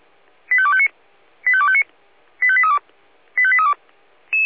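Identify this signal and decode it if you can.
Five-tone selective call (ZVEI), heard as quick runs of five tones, tones of about 70 ms. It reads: 85259 85259 85832 85832
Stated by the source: ZVEI five-tone selective call received by SDR.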